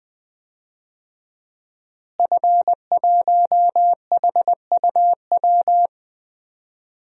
Morse code, heard as a keyed tone, about 20 wpm, 700 Hz. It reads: F1HUW
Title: DXCC-100 Call Signs - 5 Characters - Repeated 3x - 20wpm